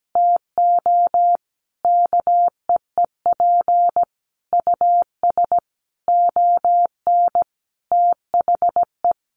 Morse code, heard as a keyed tone, about 17 wpm, 700 Hz. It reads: TO KEEP US ON THE
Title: Sentences from Top 1400 Words 17wpm